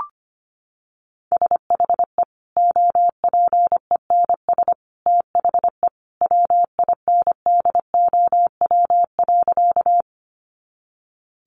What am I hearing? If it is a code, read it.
Morse code, heard as a keyed tone, about 25 wpm, 700 Hz. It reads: SHE OPENS THE WINDOW.
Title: Additional Easy Phrases 25wpm